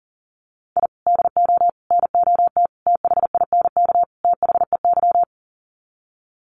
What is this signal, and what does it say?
Morse code, heard as a keyed tone, about 40 wpm, 700 Hz. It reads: I DO NOT THINK THEY
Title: Sets of 5 Words 40wpm